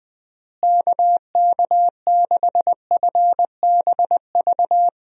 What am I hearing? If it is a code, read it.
Morse code, heard as a keyed tone, about 20 wpm, 700 Hz. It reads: KK6FBV
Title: USA Call Signs - 6 Characters - Repeated 3x - 20wpm